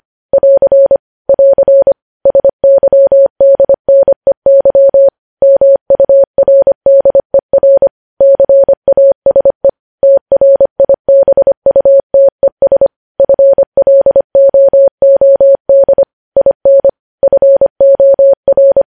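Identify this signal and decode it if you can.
Morse code, heard as a keyed tone, about 25 wpm, 550 Hz. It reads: <AR> <AR> SYDNEY MURDER CASE TRIBUTES FLOOD IN FOR
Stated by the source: Morse code news headlines sent at 25 wpm.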